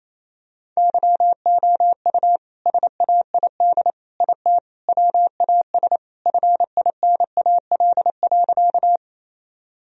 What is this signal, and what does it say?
Morse code, heard as a keyed tone, about 28 wpm, 700 Hz. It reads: YOU SAID IT WAS FINAL.
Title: Sentences from Top 1000 Words - Review Part 2 - 28wpm